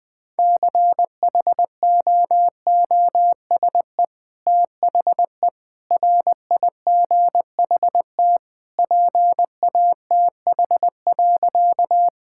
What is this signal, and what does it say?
Morse code, heard as a keyed tone, about 20 wpm, 700 Hz. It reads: CHOOSE THE RIGHT PATH.
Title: Sentences with 2-Word Phrases - 20wpm